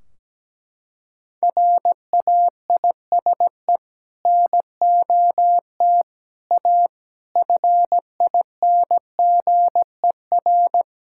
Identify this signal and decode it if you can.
Morse code, heard as a keyed tone, about 17 wpm, 700 Hz. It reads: RAISE NOT A FINGER
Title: Sentences from Top 1600 Words 17wpm